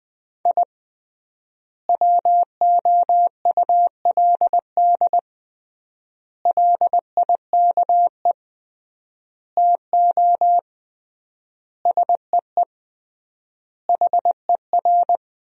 Morse code, heard as a keyed tone, about 20 wpm, 700 Hz. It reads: I WOULD LIKE TO SEE HER
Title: 3x Word-Spacing - Sentences from Top 200 Words - 20wpm